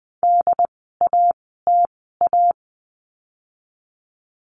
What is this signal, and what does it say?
Morse code, heard as a keyed tone, about 20 wpm, spaced as for 14 wpm, 700 Hz. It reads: DATA
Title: 075 Lesson 13f - Letter L - Words 20wpm